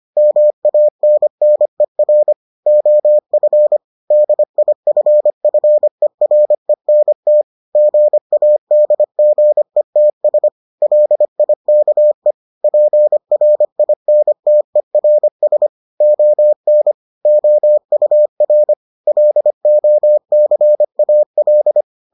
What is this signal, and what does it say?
Morse code, heard as a keyed tone, about 25 wpm, 600 Hz. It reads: MANNER OF DIFFERENT GADGETS LIKE PRINTERS ON OUR LOCAL